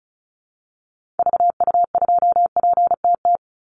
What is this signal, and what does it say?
Morse code, heard as a keyed tone, about 34 wpm, 700 Hz. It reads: VU2PTT
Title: Call Signs - 35wpm